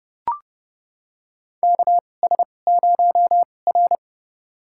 Morse code, heard as a keyed tone, about 30 wpm, spaced as for 22 wpm, 700 Hz. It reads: KS0R
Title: USA Call Signs - 2x Character Spacing - 30wpm